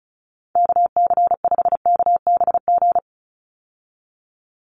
Morse code, heard as a keyed tone, about 36 wpm, 700 Hz. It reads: KC5KBG
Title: USA Call Signs 35wpm